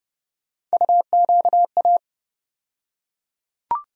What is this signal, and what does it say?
Morse code, heard as a keyed tone, about 31 wpm, 700 Hz. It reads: UQA
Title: USA Call Sign Postfixes - 3 Letters 30wpm